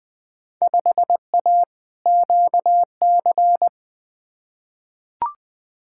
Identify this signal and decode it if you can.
Morse code, heard as a keyed tone, about 20 wpm, 700 Hz. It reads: 5A QC
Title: Field Day Exchanges Encore - 20wpm